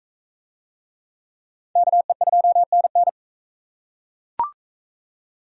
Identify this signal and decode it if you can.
Morse code, heard as a keyed tone, about 41 wpm, 700 Hz. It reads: KE2NN